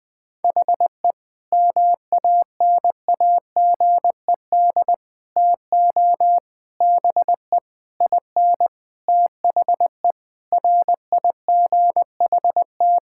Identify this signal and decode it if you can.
Morse code, heard as a keyed tone, about 21 wpm, 700 Hz. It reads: HE MANAGED TO BE IN THE RIGHT
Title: Sentences with 9-Word Phrases - 20wpm